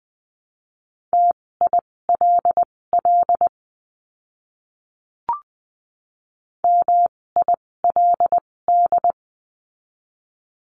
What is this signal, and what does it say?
Morse code, heard as a keyed tone, about 20 wpm, spaced as for 16 wpm, 700 Hz. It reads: TILL MILD
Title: 256 Lesson 42a - Words 12-20wpm